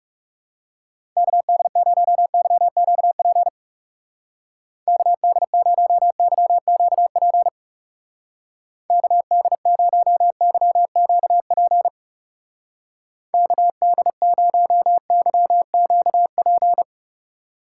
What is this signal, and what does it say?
Morse code, heard as a keyed tone, about 36 wpm, 700 Hz. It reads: KD0YQP KD0YQP KD0YQP KD0YQP
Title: Speed Racing - USA 2x3 Call Signs 30wpm